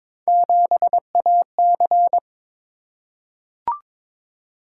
Morse code, heard as a keyed tone, about 22 wpm, 700 Hz. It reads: 7AC
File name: practice / 22wpm / Three Letter-Number Combinations - 22wpm